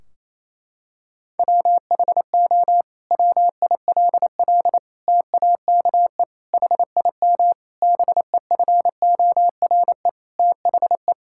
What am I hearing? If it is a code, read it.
Morse code, heard as a keyed tone, about 28 wpm, 700 Hz. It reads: WHO WILL TAKE HIM BEFORE THE
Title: Sentences from Top 1600 Words 28wpm